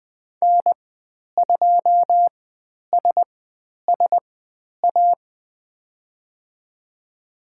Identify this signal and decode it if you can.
Morse code, heard as a keyed tone, about 20 wpm, spaced as for 10 wpm, 700 Hz. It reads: N2SSA
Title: 109 Lesson 18g - Letter m - Call Signs 20wpm